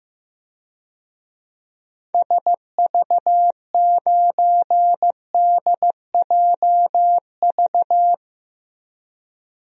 Morse code, heard as a keyed tone, about 15 wpm, 700 Hz. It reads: SV9DJV
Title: DXCC-100 Call Signs - 6 Characters - Repeated 2x - 15wpm